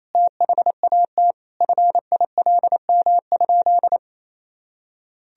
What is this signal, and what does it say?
Morse code, heard as a keyed tone, about 28 wpm, 700 Hz. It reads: THAT FILM?
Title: Sentences from Top 1000 Words - Review Part 3 - 28wpm